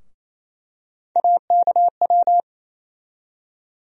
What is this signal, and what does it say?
Morse code, heard as a keyed tone, about 28 wpm, 700 Hz. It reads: AKW
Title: USA Call Sign Postfixes - 3 Letters 28wpm